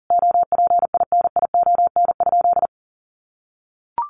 Morse code, heard as a keyed tone, about 40 wpm, 700 Hz. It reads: OPINION?
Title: Sentences from Top 1200 Words 40wpm